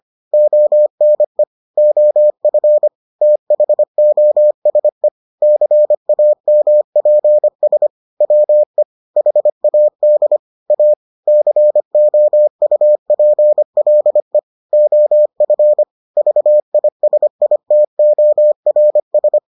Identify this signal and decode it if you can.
Morse code, heard as a keyed tone, about 25 wpm, 600 Hz. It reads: ONE OF THOSE CAMPS WE HAD A COUPLE OF VISITORS